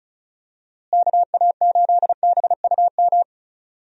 Morse code, heard as a keyed tone, about 34 wpm, 700 Hz. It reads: KA8BUM
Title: USA Call Signs - 6 Characters - Repeated 3x - 35wpm